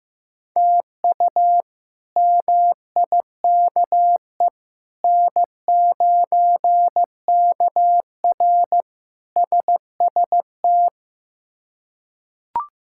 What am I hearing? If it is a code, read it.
Morse code, heard as a keyed tone, about 16 wpm, 700 Hz. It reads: TU MIKE N9KR SST
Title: K1USN SST QSO Practice - 15wpm